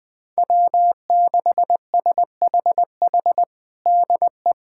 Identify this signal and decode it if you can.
Morse code, heard as a keyed tone, about 20 wpm, 700 Hz. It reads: W6SHH DE
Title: USA Callsign DE Callsign - 5 Characters - 20wpm